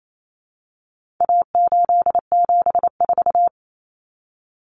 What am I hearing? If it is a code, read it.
Morse code, heard as a keyed tone, about 28 wpm, 700 Hz. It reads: A874
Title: Common Cut Numbers - 4-Digit 28wpm